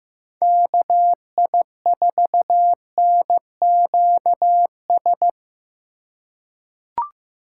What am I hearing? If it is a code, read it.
Morse code, heard as a keyed tone, about 15 wpm, 700 Hz. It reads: KI4NQS
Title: USA Call Signs - 2x Character Spacing - 15wpm